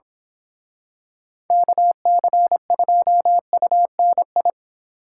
Morse code, heard as a keyed tone, about 27 wpm, 700 Hz. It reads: KC2UNI
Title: Speed Racing - USA 2x3 Call Signs 22wpm